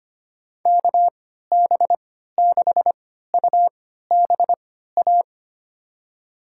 Morse code, heard as a keyed tone, about 25 wpm, spaced as for 14 wpm, 700 Hz. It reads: KB6UBA
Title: USA Call Signs - 6 Characters - 3x Character Spacing - 25wpm